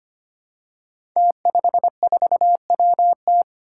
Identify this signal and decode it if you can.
Morse code, heard as a keyed tone, about 25 wpm, 700 Hz. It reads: T54WT
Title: Five Challenging Letter-Number Combinations 25wpm